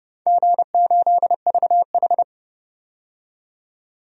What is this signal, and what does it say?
Morse code, heard as a keyed tone, about 30 wpm, 700 Hz. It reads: G8VH
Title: DXCC-100 Call Signs - 4 Characters - 30wpm